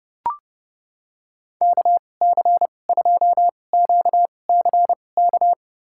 Morse code, heard as a keyed tone, about 32 wpm, spaced as for 22 wpm, 700 Hz. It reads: KC2QCK
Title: USA Call Signs - 2x Character Spacing - 30wpm